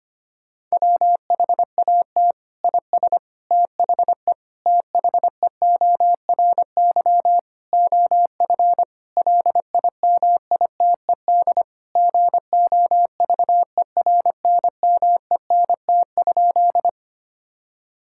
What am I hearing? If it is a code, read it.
Morse code, heard as a keyed tone, about 25 wpm, 700 Hz. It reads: WHAT IS THE THEORY OF LIMITED GOVERNMENT?
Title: Sentences from Top 1000 Words - Review Part 6 - 25wpm